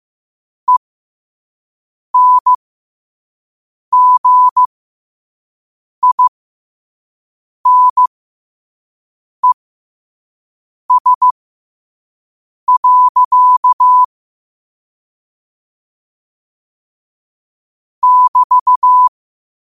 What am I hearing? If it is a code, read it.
Morse code, heard as a keyed tone, about 15 wpm, spaced as for 5 wpm, 1000 Hz. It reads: ENGINES. <BT>
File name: qotdfiles / 5wpm